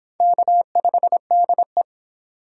Morse code, heard as a keyed tone, about 24 wpm, 700 Hz. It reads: K5DE